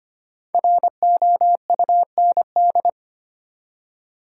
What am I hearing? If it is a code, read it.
Morse code, heard as a keyed tone, about 25 wpm, 700 Hz. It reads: ROUND
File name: 5 Letter Words - First 3 Letters Given - 25wpm